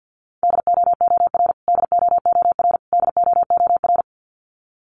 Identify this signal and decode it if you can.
Morse code, heard as a keyed tone, about 50 wpm, 700 Hz. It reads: DOOR DOOR DOOR
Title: Top 500 Words Repeated 50wpm